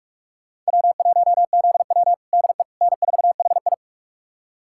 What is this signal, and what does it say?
Morse code, heard as a keyed tone, about 44 wpm, 700 Hz. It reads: W1ZW DE N4HI
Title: USA Callsign DE Callsign - 4 Characters - 45wpm